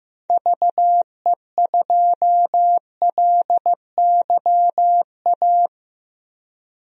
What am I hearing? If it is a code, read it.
Morse code, heard as a keyed tone, about 16 wpm, 700 Hz. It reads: VE2LYA
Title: DXCC-100 Call Signs - 6 Characters - Repeated 2x - 15wpm